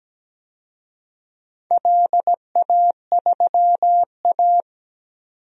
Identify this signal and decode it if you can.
Morse code, heard as a keyed tone, about 17 wpm, 700 Hz. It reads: LA3A
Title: DXCC-100 Call Signs - 4 Characters - 17wpm